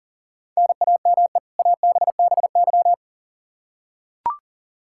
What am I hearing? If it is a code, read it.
Morse code, heard as a keyed tone, about 40 wpm, 700 Hz. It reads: NAME ABBY
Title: Practice QSO - Exchange 1 - Beginning - 40wpm